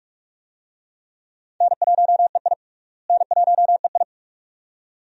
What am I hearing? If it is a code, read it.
Morse code, heard as a keyed tone, about 46 wpm, 700 Hz. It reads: N1EI N1EI
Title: USA Call Signs - 4 Characters - Repeated 2x - 45wpm